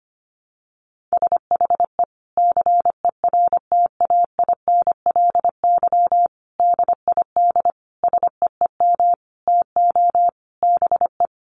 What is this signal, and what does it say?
Morse code, heard as a keyed tone, about 25 wpm, 700 Hz. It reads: SHE CERTAINLY DID SEEM TO BE
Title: Sentences from Top 1100 Words 25wpm